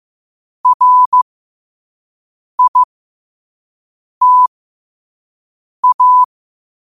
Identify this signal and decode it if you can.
Morse code, heard as a keyed tone, about 15 wpm, spaced as for 5 wpm, 1000 Hz. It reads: RITA
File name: qotdfiles / 5wpm